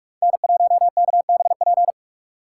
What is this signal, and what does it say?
Morse code, heard as a keyed tone, about 46 wpm, 700 Hz. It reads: N1KBP
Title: USA Call Signs - 3x Character Spacing - 45wpm